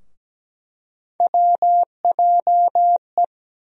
Morse code, heard as a keyed tone, about 17 wpm, 700 Hz. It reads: WJE
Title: USA Call Sign Postfixes - 3 Letters 17wpm